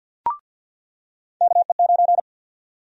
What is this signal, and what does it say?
Morse code, heard as a keyed tone, about 49 wpm, 700 Hz. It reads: KE9
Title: USA Call Sign Prefixes - 50wpm